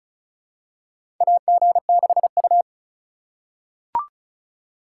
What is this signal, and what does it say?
Morse code, heard as a keyed tone, about 34 wpm, 700 Hz. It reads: AG6U
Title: USA Call Signs - 4 Characters - 4x Character Spacing - 35wpm